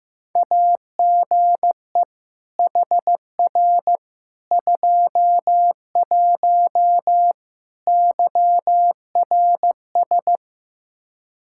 Morse code, heard as a keyed tone, about 15 wpm, 700 Hz. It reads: AGE HR 21 YRS
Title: Practice QSO - Exchange 3 - More Info - 15wpm